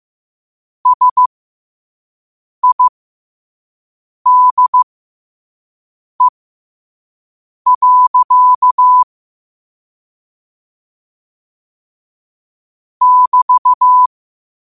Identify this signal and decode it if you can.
Morse code, heard as a keyed tone, about 15 wpm, spaced as for 5 wpm, 1000 Hz. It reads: SIDE. <BT>
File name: qotdfiles / 5wpm